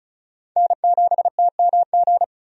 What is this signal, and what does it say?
Morse code, heard as a keyed tone, about 36 wpm, 700 Hz. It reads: N7TMG
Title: USA Call Signs - 5 Characters - Repeated 3x - 35wpm